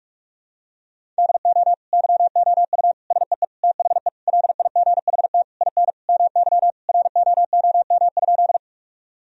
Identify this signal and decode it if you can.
Morse code, heard as a keyed tone, about 46 wpm, 700 Hz. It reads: DO YOU SEE THE LIGHT IN MY ROOM?
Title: Sentences from Top 1000 Words - Review Part 1 - 45wpm